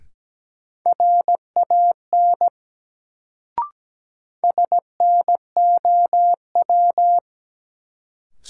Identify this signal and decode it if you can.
Morse code, heard as a keyed tone, about 17 wpm, 700 Hz. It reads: RAN SNOW
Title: Sentences from Top 500 Words - New Word Drill - 17wpm